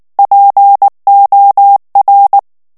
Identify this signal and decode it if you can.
Morse code, heard as a keyed tone, about 19 wpm, 800 Hz. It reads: POR
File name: morse mp3 / ESPAGNOL TEXTE EN CLAIR 2 VITESSE 18 MOTS MINUTES